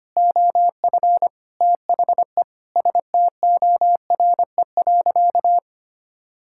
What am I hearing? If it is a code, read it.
Morse code, heard as a keyed tone, about 25 wpm, 700 Hz. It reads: OF THE STORE.
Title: Sentences from Top 1000 Words - Review Part 5 - 25wpm